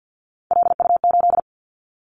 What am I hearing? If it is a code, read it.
Morse code, heard as a keyed tone, about 50 wpm, 700 Hz. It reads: LU8
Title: Callsign Prefixes for Top 100 DXCC - 50wpm